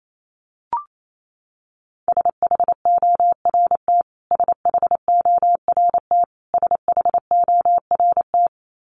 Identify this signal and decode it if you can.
Morse code, heard as a keyed tone, about 28 wpm, 700 Hz. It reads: SHORT SHORT SHORT
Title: Top 500 Words Repeated 28wpm